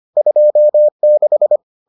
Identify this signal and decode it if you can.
Morse code, heard as a keyed tone, about 25 wpm, 600 Hz. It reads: 26